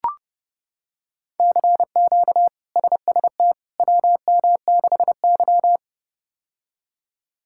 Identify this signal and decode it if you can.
Morse code, heard as a keyed tone, about 30 wpm, 700 Hz. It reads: CQ SST WM6Y
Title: K1USN SST QSO Practice - 30wpm